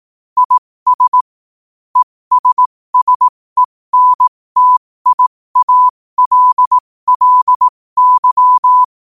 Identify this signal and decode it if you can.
Morse code, heard as a keyed tone, about 18 wpm, spaced as for 15 wpm, 1000 Hz. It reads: IS ESSENTIALLY
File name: qotdfiles / 15wpm